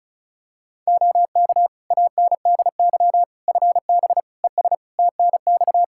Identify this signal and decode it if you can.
Morse code, heard as a keyed tone, about 34 wpm, 700 Hz. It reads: OK ANDY FB ES TNX
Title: Practice QSO - Exchange 2 - Info - 35wpm